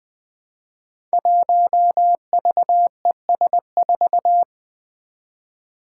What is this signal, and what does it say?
Morse code, heard as a keyed tone, about 20 wpm, 700 Hz. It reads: 1VES4
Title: Five Challenging Letter-Number Combinations 20wpm